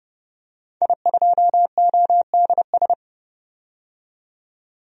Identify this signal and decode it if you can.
Morse code, heard as a keyed tone, about 30 wpm, 700 Hz. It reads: I2ODS
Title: DXCC-100 Call Signs - 5 Characters - 30wpm